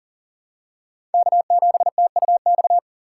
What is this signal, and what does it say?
Morse code, heard as a keyed tone, about 40 wpm, 700 Hz. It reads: K7TUX
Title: USA Call Signs - 5 Characters - 3x Character Spacing - 40wpm